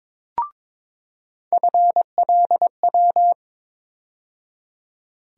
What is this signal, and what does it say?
Morse code, heard as a keyed tone, about 22 wpm, 700 Hz. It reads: FLW
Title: USA Call Sign Postfixes - 3 Letters 22wpm